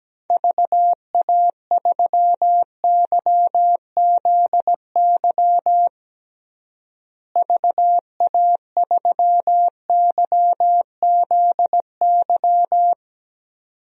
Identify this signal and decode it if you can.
Morse code, heard as a keyed tone, about 17 wpm, 700 Hz. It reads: VA3YZY VA3YZY
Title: DXCC-100 Call Signs - 6 Characters - Repeated 2x - 17wpm